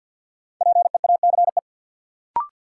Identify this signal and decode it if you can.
Morse code, heard as a keyed tone, about 52 wpm, 700 Hz. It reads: PEACE
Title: Sentences from Top 1500 Words - New Word Drill Alt - 50wpm